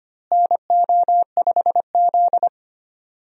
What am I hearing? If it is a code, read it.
Morse code, heard as a keyed tone, about 25 wpm, 700 Hz. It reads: NO5Z
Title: USA Call Signs - 4 Characters - Repeated 3x - 25wpm